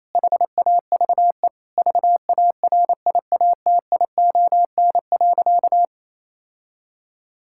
Morse code, heard as a keyed tone, about 27 wpm, 700 Hz. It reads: HAVE VARIATION.
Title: Sentences from Top 1600 Words 28wpm